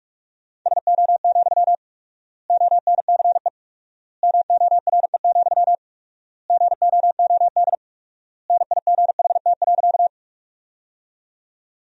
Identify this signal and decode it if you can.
Morse code, heard as a keyed tone, about 48 wpm, 700 Hz. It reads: SO, ONCE MORE, GOOD NIGHT.
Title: 4x Word-Spacing - Sentences from Top 300 Words - 45wpm